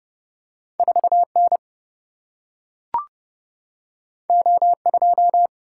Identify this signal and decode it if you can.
Morse code, heard as a keyed tone, about 31 wpm, 700 Hz. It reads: 4 N O 2
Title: Single-Letter Single-Number 30wpm